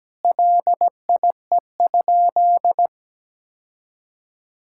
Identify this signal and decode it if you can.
Morse code, heard as a keyed tone, about 17 wpm, 700 Hz. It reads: LIE?
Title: Sentences from Top 1600 Words 17wpm